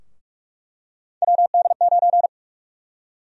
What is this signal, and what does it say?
Morse code, heard as a keyed tone, about 42 wpm, 700 Hz. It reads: WD9
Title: USA Call Sign Prefixes - 45wpm